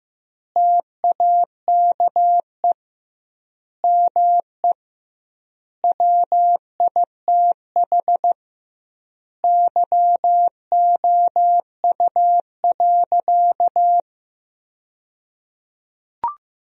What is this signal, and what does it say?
Morse code, heard as a keyed tone, about 15 wpm, 700 Hz. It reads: TAKE ME WITH YOU.
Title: 2x Word-Spacing - Sentences from Top 200 Words - 15wpm